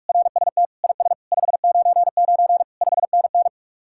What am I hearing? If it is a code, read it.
Morse code, heard as a keyed tone, about 49 wpm, 700 Hz. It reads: RST IS 599 5NN